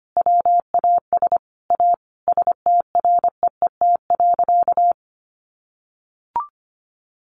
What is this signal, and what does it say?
Morse code, heard as a keyed tone, about 25 wpm, 700 Hz. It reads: WAS A STREET.
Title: Sentences from Top 1100 Words 25wpm